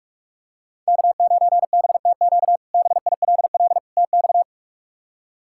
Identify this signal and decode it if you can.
Morse code, heard as a keyed tone, about 46 wpm, 700 Hz. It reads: K9BTQ BILL TX
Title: K1USN SST QSO Practice - 45wpm